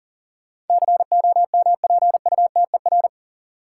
Morse code, heard as a keyed tone, about 40 wpm, 700 Hz. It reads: COMPUTER
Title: Sentences from Top 1000 Words - Review Part 3 - New Word Drill - 40wpm